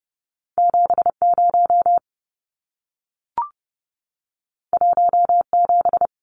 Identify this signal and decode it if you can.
Morse code, heard as a keyed tone, about 30 wpm, 700 Hz. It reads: 70 17
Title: Numbers - 2-Digit 30wpm